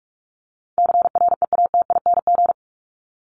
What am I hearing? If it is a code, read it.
Morse code, heard as a keyed tone, about 44 wpm, 700 Hz. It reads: CREATING
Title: ING Suffix - Top 100 Words with Base Words 45wpm